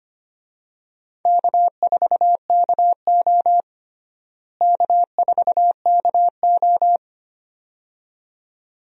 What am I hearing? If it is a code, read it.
Morse code, heard as a keyed tone, about 25 wpm, 700 Hz. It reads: K4KO K4KO
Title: USA Call Signs - 4 Characters - Repeated 2x - 25wpm